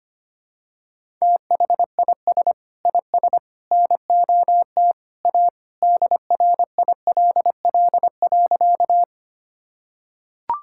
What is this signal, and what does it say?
Morse code, heard as a keyed tone, about 25 wpm, 700 Hz. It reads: THIS IS NOT A DRILL.
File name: practice / 25wpm / Sentences with 3-Word Phrases - 25wpm